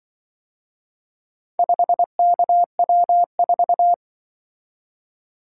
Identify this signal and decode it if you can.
Morse code, heard as a keyed tone, about 25 wpm, 700 Hz. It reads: 5KW4